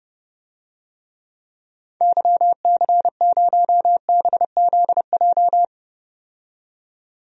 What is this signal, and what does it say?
Morse code, heard as a keyed tone, about 30 wpm, 700 Hz. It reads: YC0BZJ
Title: DXCC-100 Call Signs - 6 Characters - 30wpm